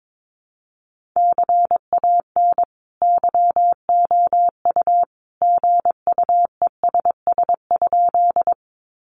Morse code, heard as a keyed tone, about 22 wpm, 700 Hz. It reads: CAN YOU GUESS?
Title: Sentences from Top 1500 Words 22wpm